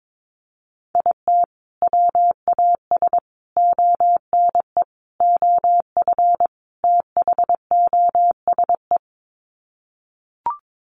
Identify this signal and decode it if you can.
Morse code, heard as a keyed tone, about 22 wpm, 700 Hz. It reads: IT WAS ONE OF THOSE
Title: Sets of 5 Words Encore 22wpm